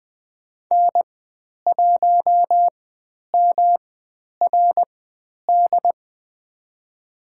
Morse code, heard as a keyed tone, about 20 wpm, spaced as for 10 wpm, 700 Hz. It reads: N1MRD
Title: 109 Lesson 18g - Letter m - Call Signs 20wpm